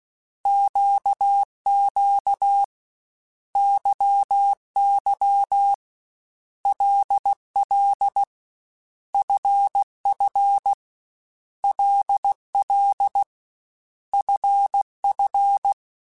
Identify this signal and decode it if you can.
Morse code, heard as a keyed tone, about 16 wpm, 800 Hz. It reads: QQ YY LL FF LL FF
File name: Basic Letter Groups By7Groups / LFCYQ / LFCYQ@13